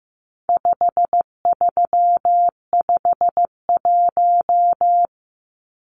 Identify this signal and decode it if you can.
Morse code, heard as a keyed tone, about 15 wpm, 700 Hz. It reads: 5351
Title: Numbers - 4-Digit 15wpm